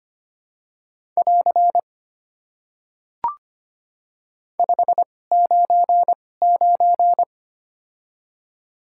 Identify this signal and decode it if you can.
Morse code, heard as a keyed tone, about 25 wpm, spaced as for 18 wpm, 700 Hz. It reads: <AR> 599
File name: Top 90 QSO Elements - 2x Character Spacing - 25wpm